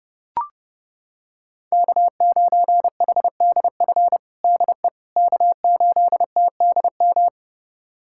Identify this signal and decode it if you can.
Morse code, heard as a keyed tone, about 30 wpm, 700 Hz. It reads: K9HDF DE K8TDM
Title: USA Callsign DE Callsign - 5 Characters - 30wpm